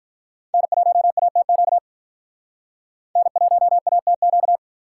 Morse code, heard as a keyed tone, about 43 wpm, 700 Hz. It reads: N1ATQ N1ATQ
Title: Speed Racing - USA Call Signs 35wpm